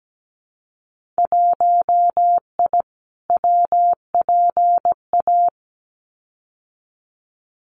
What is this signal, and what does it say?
Morse code, heard as a keyed tone, about 17 wpm, 700 Hz. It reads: 1I WPA
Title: Winter Field Day Exchanges Encore 17wpm